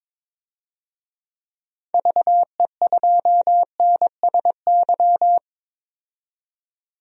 Morse code, heard as a keyed tone, about 22 wpm, 700 Hz. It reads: VE2NSY
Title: DXCC-100 Call Signs - 6 Characters - 22wpm